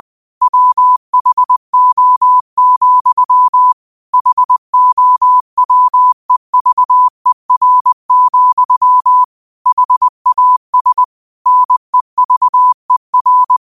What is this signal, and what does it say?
Morse code, heard as a keyed tone, about 20 wpm, 1000 Hz. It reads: WHO, HOWEVER, HAS NEVER
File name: qotdfiles / 20wpm